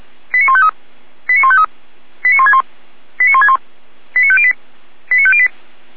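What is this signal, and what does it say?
Five-tone selective call (ZVEI), heard as quick runs of five tones, tones of about 80 ms. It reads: 89153 89153 89161 89161 89508 89508
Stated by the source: ZVEI five-tone selective call 89099 received on SDR.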